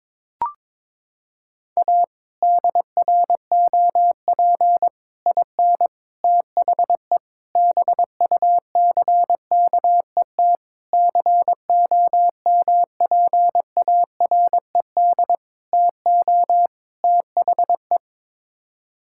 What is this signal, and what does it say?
Morse code, heard as a keyed tone, about 22 wpm, 700 Hz. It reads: A DROP IN THE BUCKET COMPARED TO THE
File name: practice / 22wpm / Sets of 8 Words - 22wpm